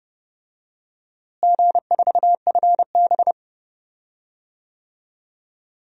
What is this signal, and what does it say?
Morse code, heard as a keyed tone, about 30 wpm, 700 Hz. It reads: G4FB
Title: DXCC-100 Call Signs - 4 Characters - Repeated 2x - 30wpm